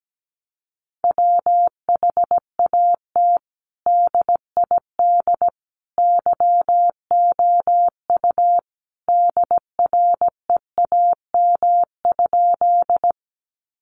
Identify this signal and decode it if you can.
Morse code, heard as a keyed tone, about 17 wpm, 700 Hz. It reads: WHAT DID YOU DREAM?
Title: Sentences from Top 1500 Words 17wpm